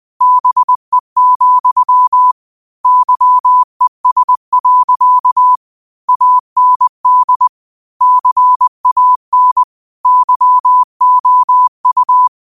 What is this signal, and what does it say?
Morse code, heard as a keyed tone, about 20 wpm, 1000 Hz. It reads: BE, YES. AND CAN YOU
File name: qotdfiles / 20wpm